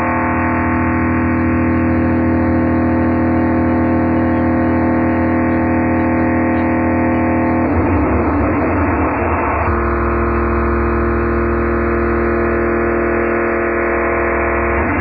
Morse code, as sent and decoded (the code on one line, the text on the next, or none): none